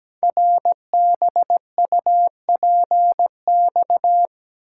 Morse code, heard as a keyed tone, about 17 wpm, 700 Hz. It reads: RBUPX
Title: Code Groups - 5 Letters - 17wpm